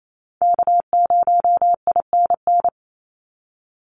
Morse code, heard as a keyed tone, about 28 wpm, 700 Hz. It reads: K0INN
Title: Call Signs Repeated 2x - 28wpm